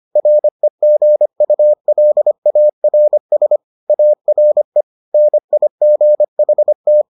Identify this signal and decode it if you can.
Morse code, heard as a keyed tone, about 25 wpm, 600 Hz. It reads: REGULARS ARE NIGHT